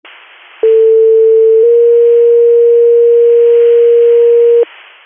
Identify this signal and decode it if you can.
Two-tone page, heard as a one-second tone, then a three-second tone, about 450 then 470 Hz. It is heard from 0.6 to 4.6 s.